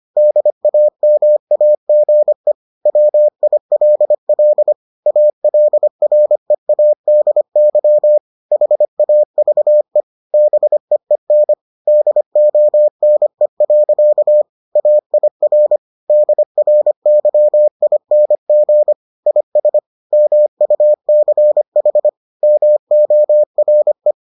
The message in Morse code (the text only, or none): DAMAGE WILL ALREADY HAVE BEEN DONE. AIR DRYING IS MUCH MORE